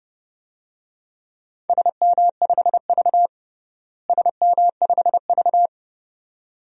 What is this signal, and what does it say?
Morse code, heard as a keyed tone, about 30 wpm, 700 Hz. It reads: SM5V SM5V
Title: DXCC-100 Call Signs - 4 Characters - Repeated 2x - 30wpm